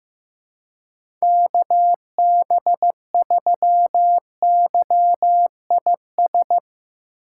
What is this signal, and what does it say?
Morse code, heard as a keyed tone, about 15 wpm, 700 Hz. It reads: KB3YIS
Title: USA Call Signs - 6 Characters - 2x Character Spacing - 15wpm